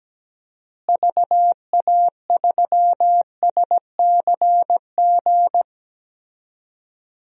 Morse code, heard as a keyed tone, about 17 wpm, 700 Hz. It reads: VA3SCG
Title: DXCC-100 Call Signs - 6 Characters - Repeated 2x - 17wpm